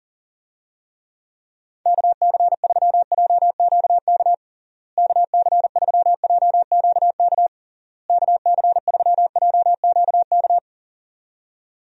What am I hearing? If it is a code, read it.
Morse code, heard as a keyed tone, about 37 wpm, 700 Hz. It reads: KC3JQK KC3JQK KC3JQK
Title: USA Call Signs - 6 Characters - Repeated 3x - 40wpm